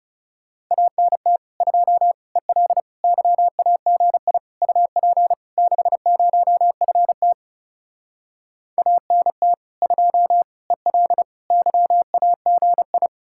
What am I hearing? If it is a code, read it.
Morse code, heard as a keyed tone, about 40 wpm, 700 Hz. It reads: ANT 2 EL YAGI UP 60FT ANT 2 EL YAGI